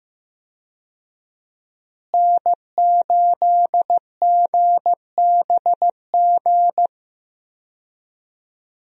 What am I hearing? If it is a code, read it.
Morse code, heard as a keyed tone, about 15 wpm, 700 Hz. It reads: N8GBG
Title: USA Call Signs - 5 Characters - Repeated 3x - 15wpm